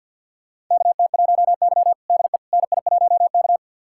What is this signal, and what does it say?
Morse code, heard as a keyed tone, about 46 wpm, 700 Hz. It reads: KT1Y DE NI1K
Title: USA Callsign DE Callsign - 4 Characters - 50wpm